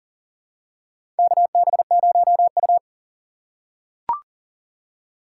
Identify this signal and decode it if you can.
Morse code, heard as a keyed tone, about 40 wpm, 700 Hz. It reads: KB0U
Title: USA Call Signs - 4 Characters - 3x Character Spacing - 40wpm